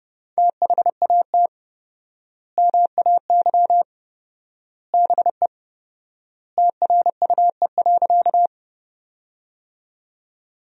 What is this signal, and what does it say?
Morse code, heard as a keyed tone, about 30 wpm, 700 Hz. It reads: THAT MAY BE TRUE.
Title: 4x Word-Spacing - Sentences from Top 400 Words - 30wpm